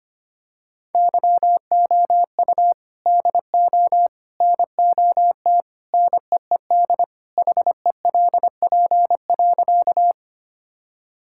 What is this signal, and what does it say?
Morse code, heard as a keyed tone, about 25 wpm, 700 Hz. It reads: YOU DO NOT NEED HELP.
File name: practice / 25wpm / Sentences with 3-Word Phrases - 25wpm